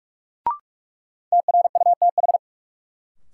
Morse code, heard as a keyed tone, about 43 wpm, 700 Hz. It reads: TRUTH
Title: Sentences from Top 800 Words - New Word Drill - 45wpm